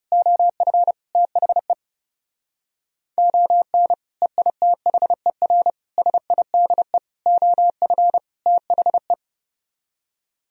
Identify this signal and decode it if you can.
Morse code, heard as a keyed tone, about 36 wpm, 700 Hz. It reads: OF THE ON EITHER SIDE OF THE
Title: Speed-Racing - Sets of 5 Words 30wpm